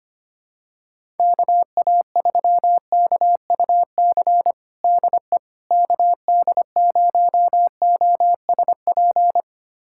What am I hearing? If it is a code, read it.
Morse code, heard as a keyed tone, about 25 wpm, 700 Hz. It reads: KA3KUC DE KD0OSP
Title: USA Callsign DE Callsign 25wpm